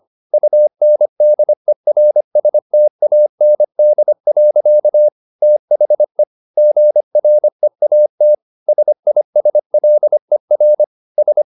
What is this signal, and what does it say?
Morse code, heard as a keyed tone, about 25 wpm, 600 Hz. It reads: UNDERSTAND. THE GREAT SISLER S
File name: the old man and the sea part 1 25wpm